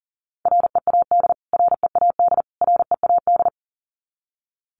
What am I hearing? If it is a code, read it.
Morse code, heard as a keyed tone, about 37 wpm, 700 Hz. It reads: READ READ READ
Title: Top 500 Words Repeated 40wpm